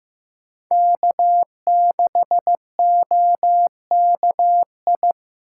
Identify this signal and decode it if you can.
Morse code, heard as a keyed tone, about 18 wpm, 700 Hz. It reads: K6OKI
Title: Speed Racing - USA 1x3 Call Signs 15wpm